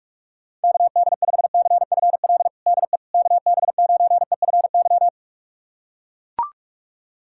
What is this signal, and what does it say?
Morse code, heard as a keyed tone, about 42 wpm, 700 Hz. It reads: KD5CFL DE KB9EFY
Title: USA Callsign DE Callsign 45wpm